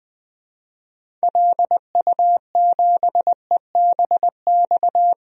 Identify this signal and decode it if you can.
Morse code, heard as a keyed tone, about 21 wpm, 700 Hz. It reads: LU7EBX
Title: DXCC-100 Call Signs - 6 Characters - 20wpm